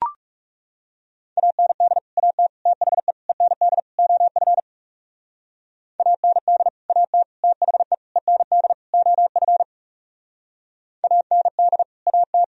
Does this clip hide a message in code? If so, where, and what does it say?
Yes: Morse code from 1.4 to 12.4 s, heard as a keyed tone, about 38 wpm, 700 Hz. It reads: AND AT THE END OF AND AT THE END OF AND AT